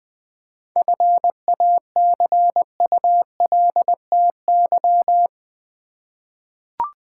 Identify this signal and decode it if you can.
Morse code, heard as a keyed tone, about 20 wpm, 700 Hz. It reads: FACULTY
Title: 7 Letter Words - First 3 Letters Given - 20wpm